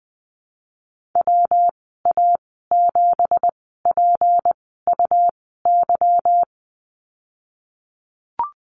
Morse code, heard as a keyed tone, about 19 wpm, spaced as for 14 wpm, 700 Hz. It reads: WA7PUY
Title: 213 Lesson 34g - Number 8 - Call Signs 20wpm